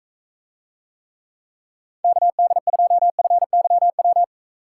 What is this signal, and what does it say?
Morse code, heard as a keyed tone, about 44 wpm, 700 Hz. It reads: KD2FYW